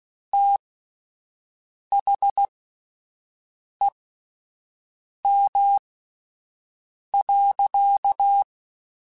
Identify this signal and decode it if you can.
Morse code, heard as a keyed tone, about 16 wpm, spaced as for 5 wpm, 800 Hz. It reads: THEM.
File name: Spaced / Chuckles Spaced-mp3s / SoftWords sp @10